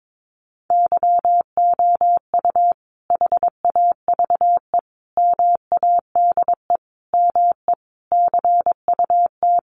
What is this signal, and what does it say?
Morse code, heard as a keyed tone, about 22 wpm, 700 Hz. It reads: YOU HAVE MADE ME CUT
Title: Sentences from Top 1300 Words 22wpm